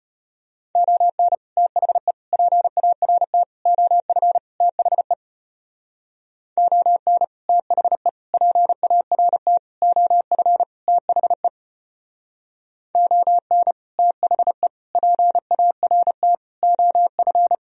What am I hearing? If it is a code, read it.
Morse code, heard as a keyed tone, about 30 wpm, 700 Hz. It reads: ON THE PART OF THE ON THE PART OF THE ON THE PART OF
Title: Speed-Racing - Sets of 5 Words 25wpm